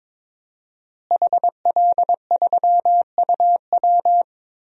Morse code, heard as a keyed tone, about 21 wpm, 700 Hz. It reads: HL3UW
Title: DXCC-100 Call Signs - 5 Characters - Repeated 2x - 22wpm